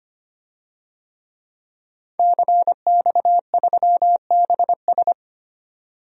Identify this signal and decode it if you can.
Morse code, heard as a keyed tone, about 25 wpm, 700 Hz. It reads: CX3BS
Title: DXCC-100 Call Signs - 5 Characters - 25wpm